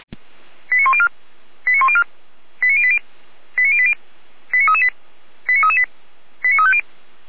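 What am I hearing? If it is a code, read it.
Five-tone selective call (ZVEI), heard as quick runs of five tones, tones of about 70 ms. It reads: 89104 89104 89080 89080 89338 89338 89350
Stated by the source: ZVEI five-tone call 89115 received by SDR.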